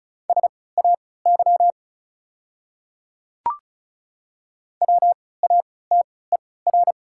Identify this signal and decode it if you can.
Morse code, heard as a keyed tone, about 36 wpm, spaced as for 20 wpm, 700 Hz. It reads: SAY WATER
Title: 3x Character Spacing - Top 200 Words - 35wpm